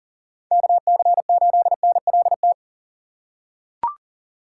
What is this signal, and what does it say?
Morse code, heard as a keyed tone, about 40 wpm, 700 Hz. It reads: KC8NLT
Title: USA Call Signs - 6 Characters - 2x Character Spacing - 40wpm